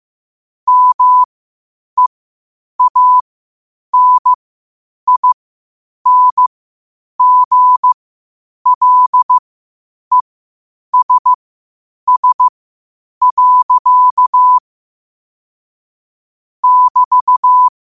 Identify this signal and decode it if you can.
Morse code, heard as a keyed tone, about 15 wpm, spaced as for 8 wpm, 1000 Hz. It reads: MEANINGLESS. <BT>
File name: qotdfiles / 8wpm